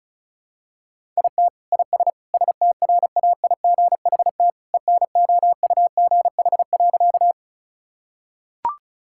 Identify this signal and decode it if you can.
Morse code, heard as a keyed tone, about 36 wpm, 700 Hz. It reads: IT IS STRAIGHT ENOUGH.
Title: Sentences from Top 1000 Words - Review Part 6 - 35wpm